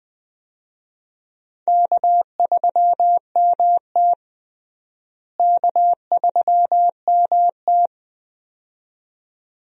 Morse code, heard as a keyed tone, about 21 wpm, 700 Hz. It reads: K3MT K3MT
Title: USA Call Signs - 4 Characters - Repeated 2x - 20wpm